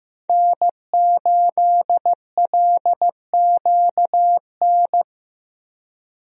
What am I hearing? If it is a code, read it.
Morse code, heard as a keyed tone, about 15 wpm, 700 Hz. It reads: N8LQN
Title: USA Call Signs - 5 Characters - Repeated 2x - 15wpm